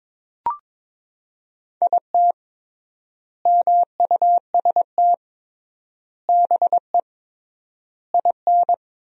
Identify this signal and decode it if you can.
Morse code, heard as a keyed tone, about 22 wpm, 700 Hz. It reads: IT MUST BE IN
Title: 3x Word-Spacing - Sentences from Top 300 Words - 22wpm